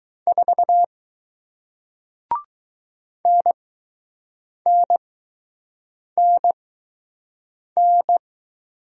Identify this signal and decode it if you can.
Morse code, heard as a keyed tone, about 20 wpm, spaced as for 6 wpm, 700 Hz. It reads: 4 NNNN